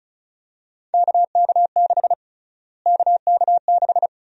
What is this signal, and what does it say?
Morse code, heard as a keyed tone, about 34 wpm, 700 Hz. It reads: KK6 KK6
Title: USA Call Sign Prefixes - Repeated 2x - 35wpm